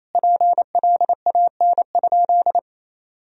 Morse code, heard as a keyed tone, about 28 wpm, 700 Hz. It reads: PLAN?
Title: Sentences from Top 1000 Words - Review Part 3 - 28wpm